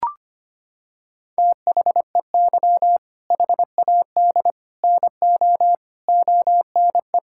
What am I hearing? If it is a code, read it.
Morse code, heard as a keyed tone, about 25 wpm, 700 Hz. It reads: THEY HAD NO ONE
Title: Sentences from Top 1600 Words 25wpm